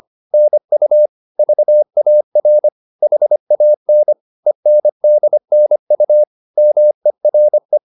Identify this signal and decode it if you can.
Morse code, heard as a keyed tone, about 25 wpm, 600 Hz. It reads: NU VAR HAN ENDNU MERE
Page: Ferie 24 wpm